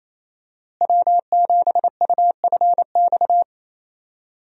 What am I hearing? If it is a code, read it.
Morse code, heard as a keyed tone, about 27 wpm, 700 Hz. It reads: W7UFX
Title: USA Call Signs - 5 Characters - Repeated 2x - 28wpm